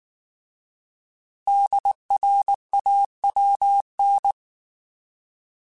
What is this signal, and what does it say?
Morse code, heard as a keyed tone, about 19 wpm, 800 Hz. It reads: DRAWN